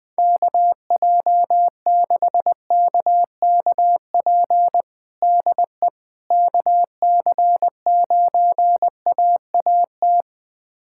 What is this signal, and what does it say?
Morse code, heard as a keyed tone, about 21 wpm, 700 Hz. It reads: KJ6KKP DE KC9AAT
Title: USA Callsign DE Callsign - 6 Characters - 20wpm